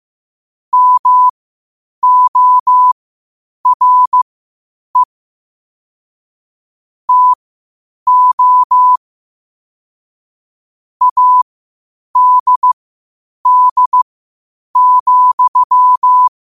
Morse code, heard as a keyed tone, about 15 wpm, spaced as for 8 wpm, 1000 Hz. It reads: MORE TO ADD,
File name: qotdfiles / 8wpm